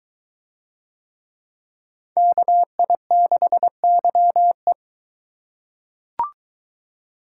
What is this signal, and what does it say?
Morse code, heard as a keyed tone, about 22 wpm, 700 Hz. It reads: KI6YE